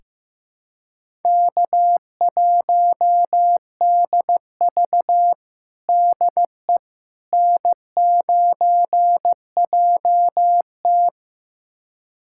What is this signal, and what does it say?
Morse code, heard as a keyed tone, about 15 wpm, 700 Hz. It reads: K1DV DE N9JT
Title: USA Callsign DE Callsign - 4 Characters - 15wpm